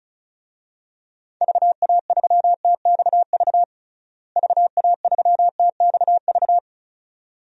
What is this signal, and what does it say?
Morse code, heard as a keyed tone, about 36 wpm, 700 Hz. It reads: VA3TXV VA3TXV
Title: DXCC-100 Call Signs - 6 Characters - Repeated 2x - 35wpm